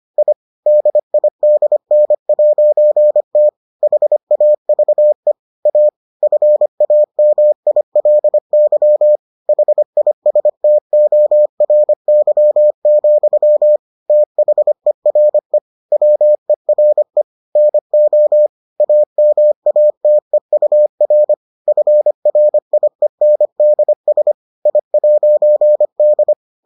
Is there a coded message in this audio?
Yes: Morse code from 0.2 to 26.3 s, heard as a keyed tone, about 25 wpm, 600 Hz. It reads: I DIDN'T HAVE A FAMILY HISTORY, THERE WERE NO AMATEUR FRIENDS I'D